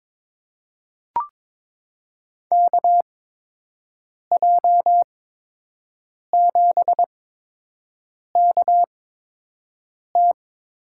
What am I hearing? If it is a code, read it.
Morse code, heard as a keyed tone, about 22 wpm, spaced as for 6 wpm, 700 Hz. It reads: KJ7KT
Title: USA Call Signs - 8x Character Spacing - 22wpm